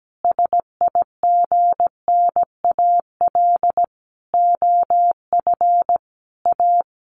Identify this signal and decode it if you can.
Morse code, heard as a keyed tone, about 17 wpm, 700 Hz. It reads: SIGNAL OF A
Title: Sentences from Top 1400 Words 17wpm